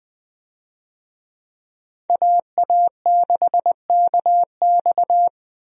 Morse code, heard as a keyed tone, about 20 wpm, 700 Hz. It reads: AA6KX
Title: USA Call Signs - 5 Characters - Repeated 2x - 20wpm